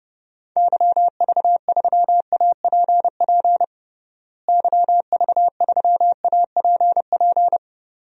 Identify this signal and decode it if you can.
Morse code, heard as a keyed tone, about 30 wpm, 700 Hz. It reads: YV3APP YV3APP
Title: DXCC-100 Call Signs - 6 Characters - Repeated 2x - 30wpm